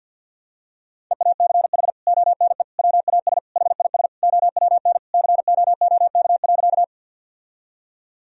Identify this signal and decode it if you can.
Morse code, heard as a keyed tone, about 50 wpm, 700 Hz. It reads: EACH ONE WAS HIS OWN COOK.
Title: Sentences from Top 1000 Words - Review Part 4 - 50wpm